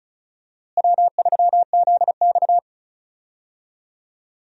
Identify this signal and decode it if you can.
Morse code, heard as a keyed tone, about 34 wpm, 700 Hz. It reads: W3ZX
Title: USA Call Signs - 4 Characters - Repeated 3x - 35wpm